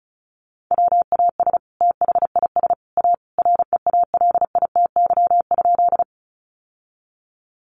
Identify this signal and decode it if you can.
Morse code, heard as a keyed tone, about 34 wpm, 700 Hz. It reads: WAS THIS A REALITY?
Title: Sentences from Top 1200 Words 35wpm